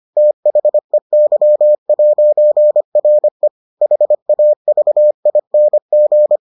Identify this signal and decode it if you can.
Morse code, heard as a keyed tone, about 25 wpm, 600 Hz. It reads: THEY'RE HAVING